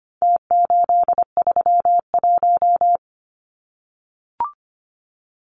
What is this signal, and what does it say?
Morse code, heard as a keyed tone, about 25 wpm, 700 Hz. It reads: T831
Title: Numbers - Contest Style - 4-Digit 25wpm